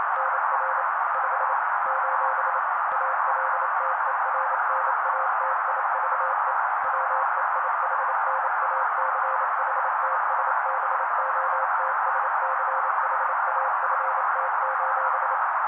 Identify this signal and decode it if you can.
Morse code, heard as a keyed tone, about 28 wpm, 530 Hz. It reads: NR 5 : ALTERNATIVE WEIHNACHTSBOTSCHAFT:
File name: daily / 27wpm noise